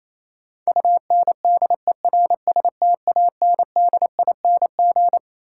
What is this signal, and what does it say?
Morse code, heard as a keyed tone, about 28 wpm, 700 Hz. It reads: UNDERSTANDING
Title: Sentences from Top 1000 Words - Review Part 3 - New Word Drill - 28wpm